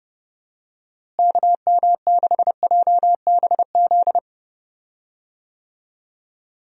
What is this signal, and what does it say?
Morse code, heard as a keyed tone, about 30 wpm, 700 Hz. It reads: KM6JBZ
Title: USA Call Signs - 6 Characters - Repeated 2x - 30wpm